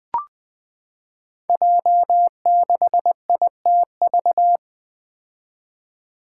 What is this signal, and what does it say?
Morse code, heard as a keyed tone, about 20 wpm, 700 Hz. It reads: J6ITV
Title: Five Challenging Letter-Number Combinations 20wpm